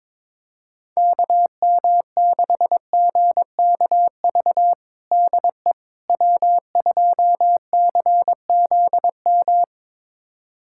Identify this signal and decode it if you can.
Morse code, heard as a keyed tone, about 22 wpm, 700 Hz. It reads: KM6GKV DE W2CZM
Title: USA Callsign DE Callsign 22wpm